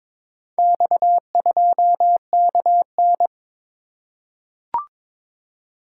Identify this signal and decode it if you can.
Morse code, heard as a keyed tone, about 22 wpm, 700 Hz. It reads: X2KN
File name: practice / 22wpm / Four Letter-Number Combinations - 22wpm